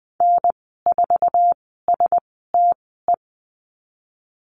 Morse code, heard as a keyed tone, about 20 wpm, spaced as for 14 wpm, 700 Hz. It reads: N4STE
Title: 062 Lesson 11g - Letter h - Call Signs 20wpm